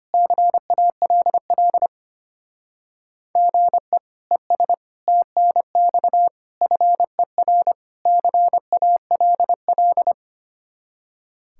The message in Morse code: CALL GE ES TNX FER CALL

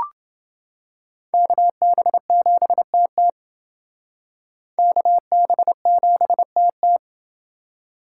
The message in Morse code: KB7TT KB7TT

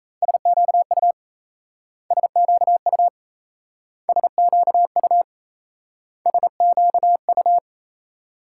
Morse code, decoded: SQU SQU SQU SQU

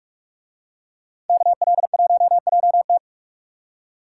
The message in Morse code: KL1JT

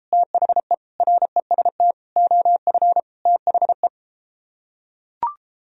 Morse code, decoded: THE REST OF THE

